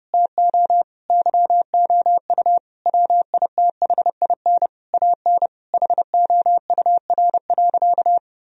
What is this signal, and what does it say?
Morse code, heard as a keyed tone, about 32 wpm, 700 Hz. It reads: TO YOU WITHIN AN HOUR.
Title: Sentences from Top 1000 Words - Review Part 4 - 30wpm